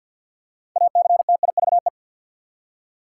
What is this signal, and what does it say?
Morse code, heard as a keyed tone, about 52 wpm, 700 Hz. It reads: ACTIVE
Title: Six Letter Words - 50wpm